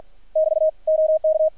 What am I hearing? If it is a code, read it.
Morse code, heard as a keyed tone, about 50 wpm, 630 Hz. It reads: <BT> OK
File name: archived / CRSnet